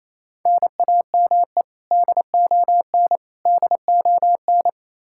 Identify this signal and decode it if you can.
Morse code, heard as a keyed tone, about 28 wpm, 700 Hz. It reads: NAME DON DON
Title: Practice QSO - Exchange 1 - Beginning - 28wpm